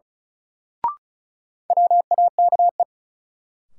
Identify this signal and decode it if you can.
Morse code, heard as a keyed tone, about 34 wpm, 700 Hz. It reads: WAKE
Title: Sets of 4 Words - New Word Drill - 35wpm